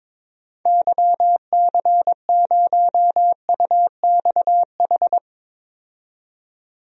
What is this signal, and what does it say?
Morse code, heard as a keyed tone, about 22 wpm, 700 Hz. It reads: YC0UXH
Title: DXCC-100 Call Signs - 6 Characters - 22wpm